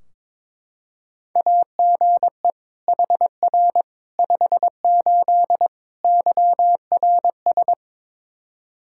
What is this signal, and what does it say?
Morse code, heard as a keyed tone, about 22 wpm, 700 Hz. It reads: AGE HR 58 YRS
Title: Practice QSO - Exchange 3 - More Info - 22wpm